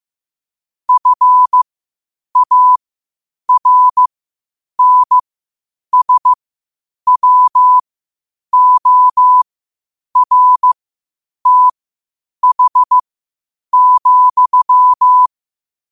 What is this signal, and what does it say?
Morse code, heard as a keyed tone, about 15 wpm, spaced as for 8 wpm, 1000 Hz. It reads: FARNSWORTH,
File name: qotdfiles / 8wpm